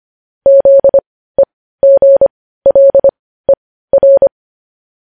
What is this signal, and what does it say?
Morse code, heard as a keyed tone, about 25 wpm, spaced as for 15 wpm, 550 Hz. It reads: ZEGLER
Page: news-headlines-in-morse-code-at-15-wpm